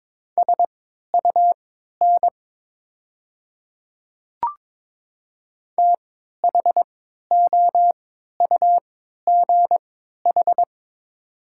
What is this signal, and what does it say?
Morse code, heard as a keyed tone, about 23 wpm, spaced as for 13 wpm, 690 Hz. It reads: SUN THOUGH
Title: 3x Character Spacing - Top 500 Words - 22wpm